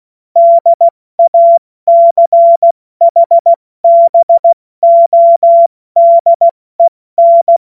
Morse code, heard as a keyed tone, about 16 wpm, 680 Hz. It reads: DACHBODEN
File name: Morsekurs / Witze / Witz 07. September 2021, 70 F80